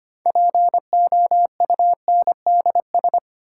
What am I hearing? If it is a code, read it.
Morse code, heard as a keyed tone, about 25 wpm, 700 Hz. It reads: POUNDS
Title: Sentences from Top 1000 Words - Review Part 2 - New Word Drill - 25wpm